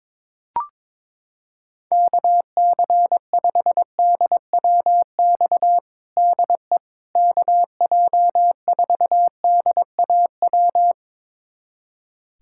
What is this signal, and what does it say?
Morse code, heard as a keyed tone, about 22 wpm, 700 Hz. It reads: KC5DWX DE KJ4DAW
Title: USA Callsign DE Callsign - 6 Characters - 22wpm